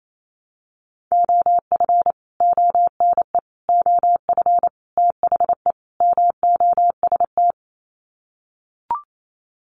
Morse code, heard as a keyed tone, about 28 wpm, 700 Hz. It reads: OF ONE OF THE MOST
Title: Sets of 5 Words Encore2 28wpm